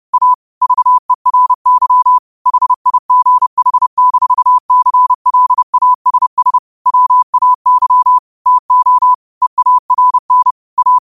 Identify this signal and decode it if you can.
Morse code, heard as a keyed tone, about 30 wpm, 1000 Hz. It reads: A VERY HIGH-CLASS WAY TO EARN A
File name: qotdfiles / 30wpm